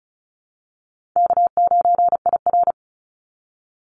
Morse code, heard as a keyed tone, about 34 wpm, 700 Hz. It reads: K9IR
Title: Call Signs Repeated 2x - 35wpm